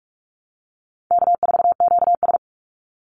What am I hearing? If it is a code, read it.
Morse code, heard as a keyed tone, about 44 wpm, 700 Hz. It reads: K4QS
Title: NA SOTA Chaser Callsigns 45wpm